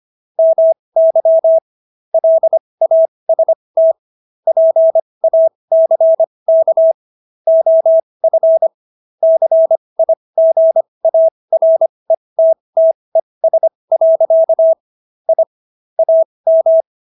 Morse code, heard as a keyed tone, about 25 wpm, spaced as for 20 wpm, 650 Hz. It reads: MY LAST PACK OF CIGARETTES. I AM